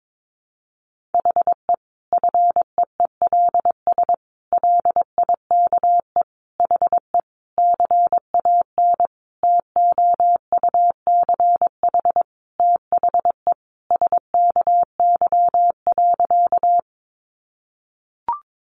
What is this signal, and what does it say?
Morse code, heard as a keyed tone, about 22 wpm, 700 Hz. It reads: HE FEELS LIKE HE CAN TOUCH THE SKY.
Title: Additional Easy Phrases 22wpm